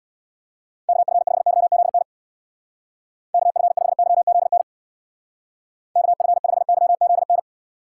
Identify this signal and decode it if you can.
Morse code, heard as a keyed tone, about 63 wpm, 700 Hz. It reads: KF5YZN KF5YZN KF5YZN